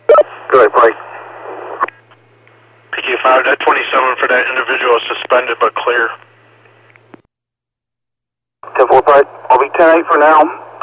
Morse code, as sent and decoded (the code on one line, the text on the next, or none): none